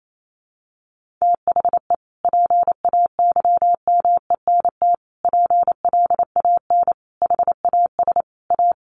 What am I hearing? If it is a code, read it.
Morse code, heard as a keyed tone, about 28 wpm, 700 Hz. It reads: THE PAYMENT PLAN HAS A
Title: Sentences from Top 1500 Words 28wpm